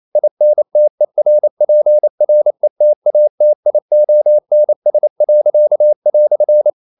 Morse code, heard as a keyed tone, about 28 wpm, 600 Hz. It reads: INTERPRETATIONS."
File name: CW / Practice / EnglishQuotations0128